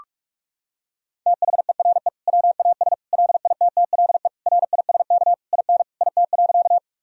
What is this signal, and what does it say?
Morse code, heard as a keyed tone, about 46 wpm, 700 Hz. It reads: THERE WAS LITTLE RISK IN IT.
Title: Sentences from Top 1000 Words - Review Part 4 - 45wpm